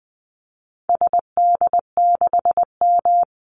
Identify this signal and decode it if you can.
Morse code, heard as a keyed tone, about 19 wpm, 700 Hz. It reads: SD6M
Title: Call Signs - 20wpm